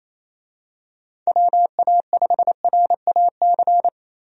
Morse code, heard as a keyed tone, about 28 wpm, 700 Hz. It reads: WA5RAC
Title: USA Call Signs - 6 Characters - 28wpm